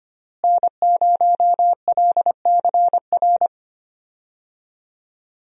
Morse code, heard as a keyed tone, about 25 wpm, 700 Hz. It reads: N0LCR
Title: USA Call Signs - 5 Characters - Repeated 2x - 25wpm